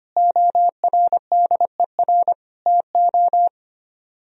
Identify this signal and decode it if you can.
Morse code, heard as a keyed tone, about 31 wpm, 700 Hz. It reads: ORDER TO
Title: Speed-Racing - Sets of 3 Words 25wpm